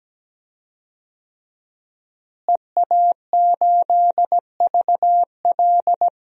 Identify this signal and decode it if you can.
Morse code, heard as a keyed tone, about 17 wpm, 700 Hz. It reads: EA8VL
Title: DXCC-100 Call Signs - 5 Characters - Repeated 3x - 17wpm